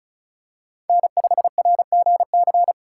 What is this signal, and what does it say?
Morse code, heard as a keyed tone, about 34 wpm, 700 Hz. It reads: N5RGC
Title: USA Call Signs - 5 Characters - 3x Character Spacing - 35wpm